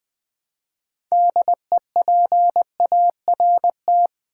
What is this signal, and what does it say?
Morse code, heard as a keyed tone, about 20 wpm, 700 Hz. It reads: DEPART
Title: Build A Word - 3 Letters At A Time - 6-Letter Words - 20wpm